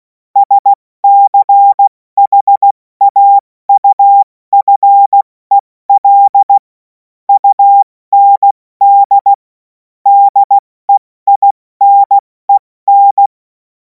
Morse code, heard as a keyed tone, about 16 wpm, 800 Hz. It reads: SCHAUFEL UND DEINEN
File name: Morsekurs / Witze / Witz 03. März 2026 70 F80